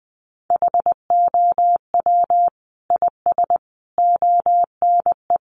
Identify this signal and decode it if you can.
Morse code, heard as a keyed tone, about 20 wpm, 700 Hz. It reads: HOW IS ONE
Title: Sentences from Top 1100 Words 20wpm